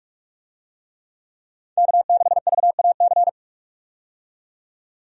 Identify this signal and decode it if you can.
Morse code, heard as a keyed tone, about 46 wpm, 700 Hz. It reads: K6VAC
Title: USA Call Signs - 5 Characters - 45wpm